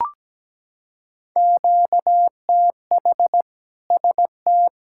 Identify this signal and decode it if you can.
Morse code, heard as a keyed tone, about 17 wpm, 700 Hz. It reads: QTH ST
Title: Practice QSO - Exchange 1 - Beginning - 17wpm